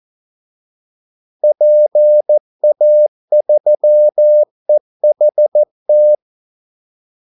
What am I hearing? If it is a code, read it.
Morse code, heard as a keyed tone, about 14 wpm, 600 Hz. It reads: PA3EHT
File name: Callsigns-14wpm